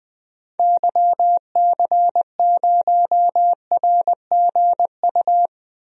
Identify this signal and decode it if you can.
Morse code, heard as a keyed tone, about 20 wpm, 700 Hz. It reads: YC0RGU
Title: DXCC-100 Call Signs - 6 Characters - Repeated 2x - 20wpm